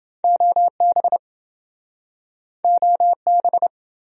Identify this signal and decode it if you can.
Morse code, heard as a keyed tone, about 28 wpm, 700 Hz. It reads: OB OB